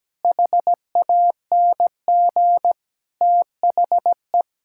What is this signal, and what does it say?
Morse code, heard as a keyed tone, about 17 wpm, 700 Hz. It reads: HANG THE
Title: Sentences from Top 1600 Words 17wpm